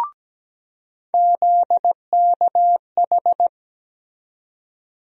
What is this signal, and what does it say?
Morse code, heard as a keyed tone, about 17 wpm, 700 Hz. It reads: ZKH
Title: USA Call Sign Postfixes - 3 Letters 17wpm